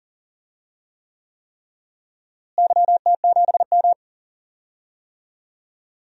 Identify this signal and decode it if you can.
Morse code, heard as a keyed tone, about 40 wpm, 700 Hz. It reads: YT7M
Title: DXCC-100 Call Signs - 4 Characters - Repeated 3x - 40wpm